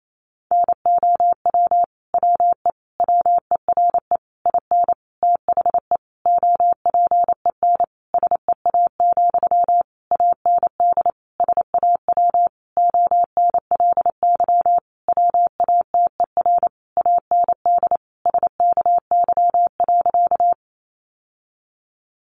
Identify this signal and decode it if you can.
Morse code, heard as a keyed tone, about 28 wpm, 700 Hz. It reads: NOW WE WERE IN THE OPEN SEA, AND SAW ONLY WATER AND SKY.
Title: Sentences from Top 1200 Words 28wpm